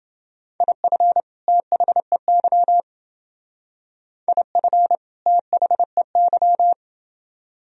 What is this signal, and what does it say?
Morse code, heard as a keyed tone, about 28 wpm, 700 Hz. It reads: IF THEY IF THEY